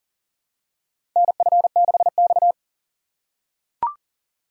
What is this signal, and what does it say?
Morse code, heard as a keyed tone, about 40 wpm, 700 Hz. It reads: NF6X
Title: USA Call Signs - 3x Character Spacing - 40wpm